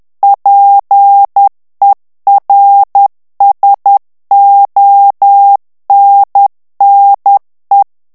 Morse code, heard as a keyed tone, about 10 wpm, 800 Hz. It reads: PERSONNE
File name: morse mp3 / FRANCAIS TEXTE CLAIR 2 VITESSE 10 MOTS MINUTES